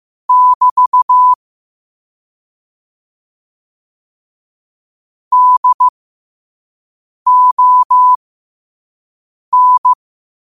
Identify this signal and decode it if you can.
Morse code, heard as a keyed tone, about 15 wpm, spaced as for 5 wpm, 1000 Hz. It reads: <BT> DON
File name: qotdfiles / 5wpm